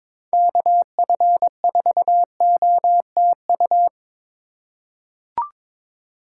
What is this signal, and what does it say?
Morse code, heard as a keyed tone, about 21 wpm, 700 Hz. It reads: KF4OTU
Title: USA Call Signs - 6 Characters - 2x Character Spacing - 22wpm